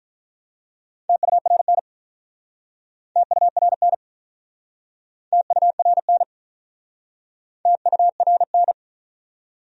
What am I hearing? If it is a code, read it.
Morse code, heard as a keyed tone, about 50 wpm, 700 Hz. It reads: TURN TURN TURN TURN